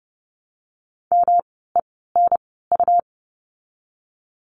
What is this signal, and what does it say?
Morse code, heard as a keyed tone, about 30 wpm, spaced as for 17 wpm, 700 Hz. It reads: MENU
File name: course / 30:10wpm / 108 Lesson 18f - Letter m - Words 30wpm